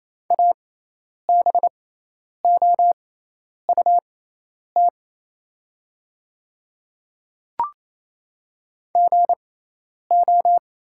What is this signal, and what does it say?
Morse code, heard as a keyed tone, about 28 wpm, spaced as for 10 wpm, 700 Hz. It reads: ABOUT GO
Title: 6x Character Spacing - Top 100 Words - 28wpm